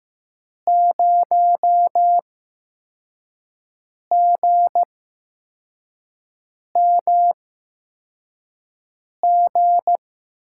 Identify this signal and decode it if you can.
Morse code, heard as a keyed tone, about 15 wpm, spaced as for 4 wpm, 700 Hz. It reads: 0GMG
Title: USA Call Signs - 6 Characters - 8x Character Spacing - 15wpm